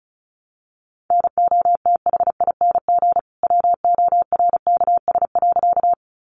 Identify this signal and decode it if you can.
Morse code, heard as a keyed tone, about 34 wpm, 700 Hz. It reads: NOTHING WORKS.
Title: Additional Easy Phrases 35wpm